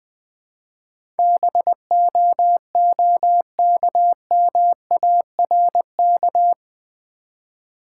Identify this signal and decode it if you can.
Morse code, heard as a keyed tone, about 20 wpm, 700 Hz. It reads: BOOKMARK
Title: Easy Words with Hints - 20wpm